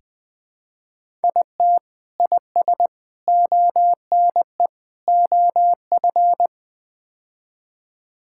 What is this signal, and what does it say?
Morse code, heard as a keyed tone, about 20 wpm, 700 Hz. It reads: IT IS ONE OF
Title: Build A Phrase - 5-Word Phrases - 20wpm